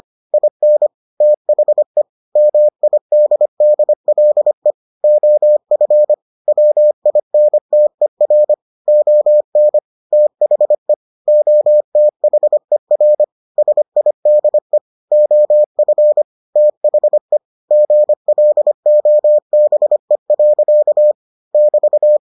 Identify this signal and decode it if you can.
Morse code, heard as a keyed tone, about 25 wpm, 600 Hz. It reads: IN THE MIDDLE OF WINTER ON THE OTHER SIDE OF THE GLOBE. <BT>